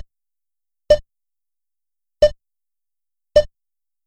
There is no code.